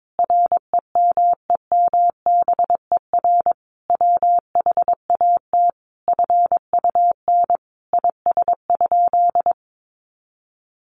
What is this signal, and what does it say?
Morse code, heard as a keyed tone, about 22 wpm, 700 Hz. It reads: REMEMBER WHAT FUN IS?
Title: Sentences from Top 700 Words 22wpm